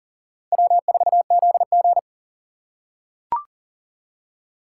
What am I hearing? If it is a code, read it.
Morse code, heard as a keyed tone, about 40 wpm, 700 Hz. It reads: W4ZG
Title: USA Call Signs - 6x Character Spacing - 40wpm